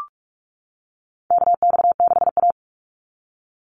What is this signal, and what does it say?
Morse code, heard as a keyed tone, about 44 wpm, 690 Hz. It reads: KX6A
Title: NA SOTA Chaser Callsigns 45wpm